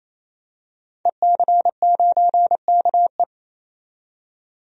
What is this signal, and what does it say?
Morse code, heard as a keyed tone, about 28 wpm, 700 Hz. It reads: EC9KE
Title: DXCC-100 Call Signs - 5 Characters - 28wpm